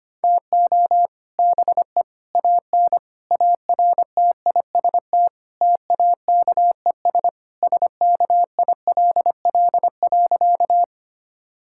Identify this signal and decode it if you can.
Morse code, heard as a keyed tone, about 25 wpm, 700 Hz. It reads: TO BE AN ARTIST TAKES SKILL.
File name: practice / 25wpm / Sentences with 3-Word Phrases - 25wpm